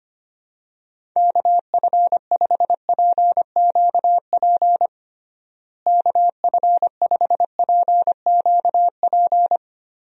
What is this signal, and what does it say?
Morse code, heard as a keyed tone, about 25 wpm, 700 Hz. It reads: KF5PQP KF5PQP
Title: USA Call Signs - 6 Characters - Repeated 2x - 25wpm